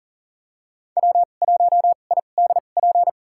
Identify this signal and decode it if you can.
Morse code, heard as a keyed tone, about 40 wpm, spaced as for 29 wpm, 700 Hz. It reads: W1IDP
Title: USA Call Signs - 2x Character Spacing - 40wpm